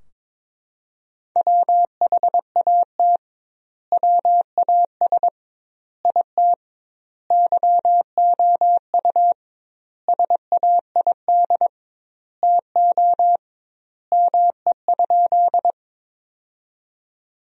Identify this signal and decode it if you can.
Morse code, heard as a keyed tone, about 21 wpm, 700 Hz. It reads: WHAT WAS IT YOU SAID TO ME?
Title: 2x Word-Spacing - Sentences from Top 200 Words - 22wpm